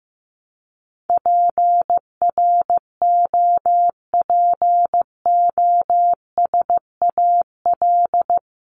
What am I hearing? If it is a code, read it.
Morse code, heard as a keyed tone, about 15 wpm, 700 Hz. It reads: PROPOSAL
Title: Top 1751-2000 Words 15wpm